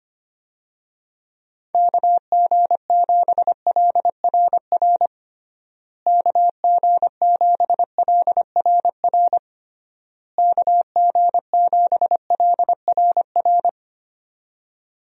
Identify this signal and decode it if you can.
Morse code, heard as a keyed tone, about 25 wpm, 700 Hz. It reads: KG7LRR KG7LRR KG7LRR
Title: USA Call Signs - 6 Characters - Repeated 3x - 25wpm